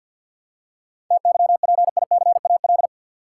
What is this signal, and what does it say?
Morse code, heard as a keyed tone, about 52 wpm, 700 Hz. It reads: TYPICAL
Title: Seven Letter Words - 50wpm